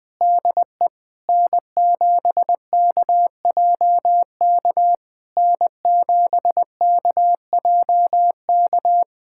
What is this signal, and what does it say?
Morse code, heard as a keyed tone, about 20 wpm, 700 Hz. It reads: DE N7KJK N7KJK
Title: SOTA QSO Practice 20wpm